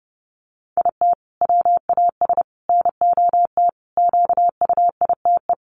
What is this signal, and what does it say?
Morse code, heard as a keyed tone, about 30 wpm, 700 Hz. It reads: IT WAS NOT QUITE
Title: Sentences from Top 700 Words 30wpm